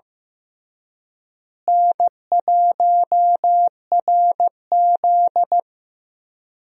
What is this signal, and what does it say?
Morse code, heard as a keyed tone, about 15 wpm, 700 Hz. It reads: N1RZ
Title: USA Call Signs - 4 Characters - Repeated 3x - 15wpm